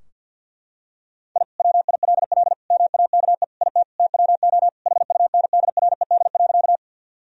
Morse code, heard as a keyed tone, about 52 wpm, 700 Hz. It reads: I WILL MAKE IT TWO HUNDRED.
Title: Sentences from Top 1000 Words - Review Part 2 - 50wpm